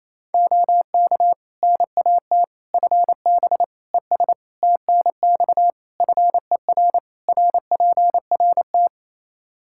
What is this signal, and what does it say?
Morse code, heard as a keyed tone, about 28 wpm, 700 Hz. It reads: OK NAT FB ES TNX FER RPRT
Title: Practice QSO - Exchange 2 - Info - 28wpm